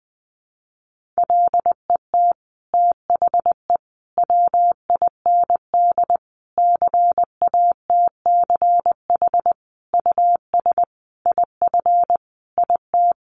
Morse code, heard as a keyed tone, about 20 wpm, 700 Hz. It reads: LET THE WIND CATCH US IF IT
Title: Sentences from Top 1500 Words 20wpm